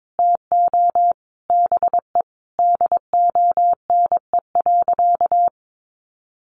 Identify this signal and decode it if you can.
Morse code, heard as a keyed tone, about 22 wpm, 700 Hz. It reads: TO BE DONE.
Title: Sentences from Top 500 Words 22wpm